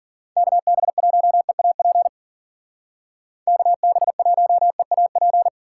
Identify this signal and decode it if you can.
Morse code, heard as a keyed tone, about 49 wpm, 700 Hz. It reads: KB1EAP KB1EAP
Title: Speed Racing - USA 2x3 Call Signs 40wpm